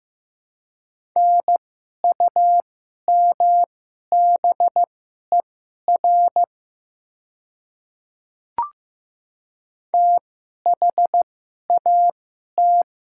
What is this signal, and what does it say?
Morse code, heard as a keyed tone, about 15 wpm, spaced as for 11 wpm, 700 Hz. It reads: NUMBER THAT
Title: 2x Character Spacing - Top 100 Words - 15wpm